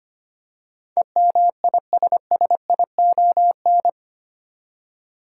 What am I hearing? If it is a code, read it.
Morse code, heard as a keyed tone, about 25 wpm, 700 Hz. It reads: EMISSION
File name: Eight Letter Words - 25wpm